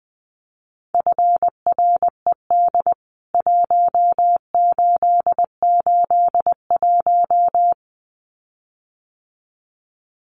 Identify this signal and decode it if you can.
Morse code, heard as a keyed tone, about 20 wpm, 700 Hz. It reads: FRED 1881
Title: CWT Exchange 20wpm